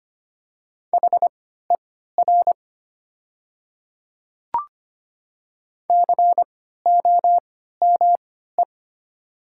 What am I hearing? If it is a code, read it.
Morse code, heard as a keyed tone, about 25 wpm, spaced as for 14 wpm, 700 Hz. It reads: HER COME
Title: 3x Character Spacing - Top 100 Words - 25wpm